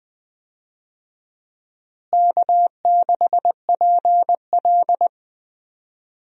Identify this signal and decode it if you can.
Morse code, heard as a keyed tone, about 21 wpm, 700 Hz. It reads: K6PL
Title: USA Call Signs - 4 Characters - Repeated 2x - 20wpm